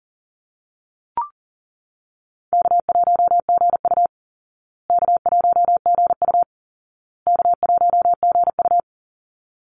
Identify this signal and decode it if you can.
Morse code, heard as a keyed tone, about 40 wpm, 700 Hz. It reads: K1GU K1GU K1GU
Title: Call Signs Repeated 2x - 40wpm